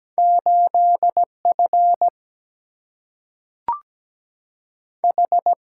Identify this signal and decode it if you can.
Morse code, heard as a keyed tone, about 17 wpm, 700 Hz. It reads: 8F H
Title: Single-Letter Single-Number 17wpm